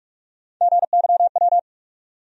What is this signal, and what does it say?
Morse code, heard as a keyed tone, about 47 wpm, 700 Hz. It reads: GYW
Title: USA Call Sign Postfixes - 3 Letters 45wpm